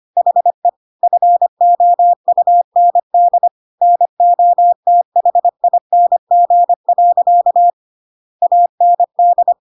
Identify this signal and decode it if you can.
Morse code, heard as a keyed tone, about 25 wpm, 700 Hz. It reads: HE FOUND NOTHING. AND
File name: Pinocchio / Pinocchio ch5 25 25 ik1hge